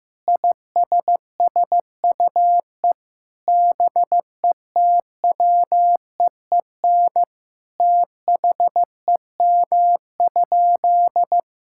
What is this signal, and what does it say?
Morse code, heard as a keyed tone, about 15 wpm, 700 Hz. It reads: ISSUE BETWEEN THEM?
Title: Sentences from Top 1000 Words - Review Part 6 - 15wpm